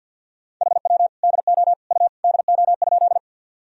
Morse code, heard as a keyed tone, about 52 wpm, 700 Hz. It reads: HW DO U DO?
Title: Sentences from Top 100 Words and Abbreviations 50wpm